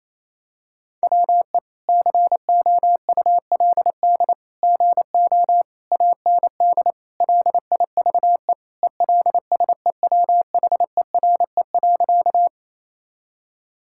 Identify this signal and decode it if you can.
Morse code, heard as a keyed tone, about 28 wpm, 700 Hz. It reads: WE COULD GO AND LIVE ELSEWHERE.
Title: Sentences from Top 1600 Words 28wpm